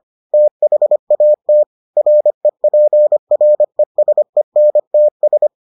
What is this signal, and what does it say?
Morse code, heard as a keyed tone, about 25 wpm, 600 Hz. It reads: THAT REPRESENTS